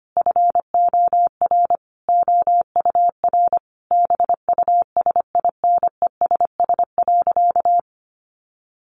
Morse code, heard as a keyed tone, about 25 wpm, 700 Hz. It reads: FOR OUR BUSINESS.
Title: Sentences from Top 1400 Words 25wpm